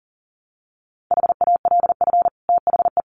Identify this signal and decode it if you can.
Morse code, heard as a keyed tone, about 40 wpm, 700 Hz. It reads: HALF THE
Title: Sentences from Top 500 Words 40wpm